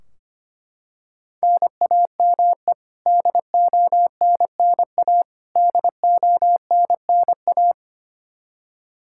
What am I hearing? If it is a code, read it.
Morse code, heard as a keyed tone, about 25 wpm, 700 Hz. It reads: NAME DONNA DONNA
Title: Practice QSO - Exchange 1 - Beginning - 25wpm